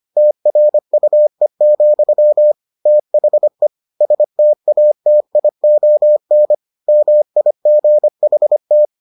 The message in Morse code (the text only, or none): TRUE, THE STATION MIGHT